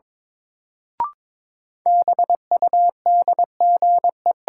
Morse code, heard as a keyed tone, about 21 wpm, 700 Hz. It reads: BUDGE
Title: Sentences from Top 1200 Words - New Word Drill - 22wpm